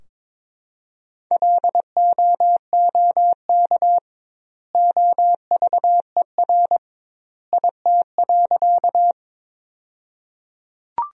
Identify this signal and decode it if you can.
Morse code, heard as a keyed tone, about 22 wpm, 700 Hz. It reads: LOOK OVER IT.
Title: 2x Word-Spacing - Sentences from Top 100 Words - 22wpm